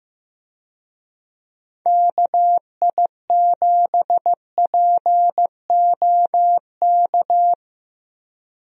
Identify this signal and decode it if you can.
Morse code, heard as a keyed tone, about 16 wpm, 700 Hz. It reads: KI7POK
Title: USA Call Signs - 6 Characters - Repeated 3x - 15wpm